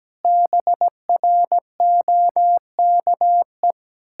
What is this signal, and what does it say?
Morse code, heard as a keyed tone, about 17 wpm, 700 Hz. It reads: BROKE
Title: Pangram Challenge v2 - 17wpm